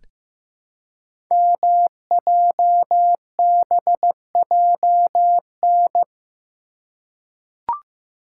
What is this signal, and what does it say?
Morse code, heard as a keyed tone, about 15 wpm, 700 Hz. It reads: MJBJN
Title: Code Groups - 5 Letters - 15wpm